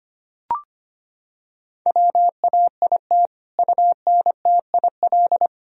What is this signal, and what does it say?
Morse code, heard as a keyed tone, about 25 wpm, 700 Hz. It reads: WAIT UNTIL
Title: Sentences from Top 1000 Words - Review Part 1 - 25wpm